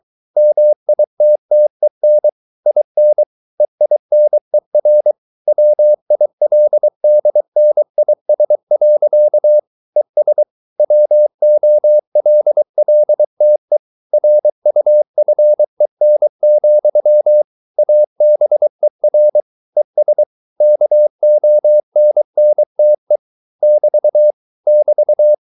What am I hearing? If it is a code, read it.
Morse code, heard as a keyed tone, about 23 wpm, 600 Hz. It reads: MITTEN IN EINER WILDNIS. ES WOLLTE RUFEN, ABER ES KONNTE <BT> <BT>